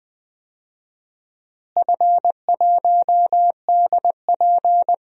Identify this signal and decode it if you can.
Morse code, heard as a keyed tone, about 20 wpm, 700 Hz. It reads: F1DP
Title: DXCC-100 Call Signs - 4 Characters - Repeated 3x - 20wpm